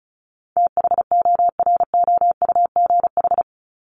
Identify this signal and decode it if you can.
Morse code, heard as a keyed tone, about 34 wpm, 700 Hz. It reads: THOROUGH
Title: LY Suffix - Top 100 Words with Base Words 35wpm